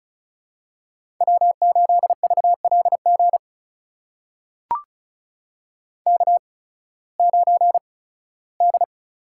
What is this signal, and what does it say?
Morse code, heard as a keyed tone, about 36 wpm, spaced as for 19 wpm, 700 Hz. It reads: W8VLG K9D
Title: USA Call Signs - 5 Characters - 8x Character Spacing - 35wpm